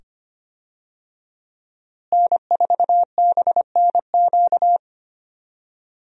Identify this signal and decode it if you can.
Morse code, heard as a keyed tone, about 25 wpm, 700 Hz. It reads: N4BNQ
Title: USA Call Signs - 5 Characters - Repeated 2x - 25wpm